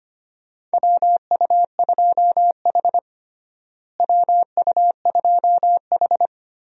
Morse code, heard as a keyed tone, about 25 wpm, 700 Hz. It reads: WU2H WU2H
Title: USA Call Signs - 4 Characters - Repeated 3x - 25wpm